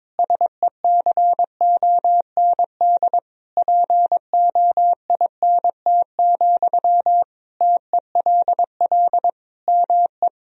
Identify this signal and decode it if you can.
Morse code, heard as a keyed tone, about 21 wpm, 700 Hz. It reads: SECOND POINT, TELL ME
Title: Sentences from Top 1000 Words - Review Part 1 - 22wpm